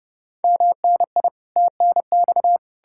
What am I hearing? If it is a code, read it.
Morse code, heard as a keyed tone, about 40 wpm, spaced as for 35 wpm, 700 Hz. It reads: MNI TNX
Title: Speed-Racing Common QSO Phrases Encore - 30wpm